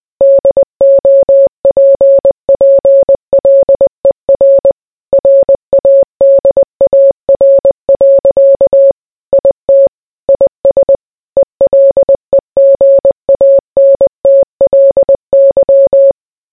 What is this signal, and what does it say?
Morse code, heard as a keyed tone, about 20 wpm, 550 Hz. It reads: DOPPLER RADAR. IT IS ELEGANTLY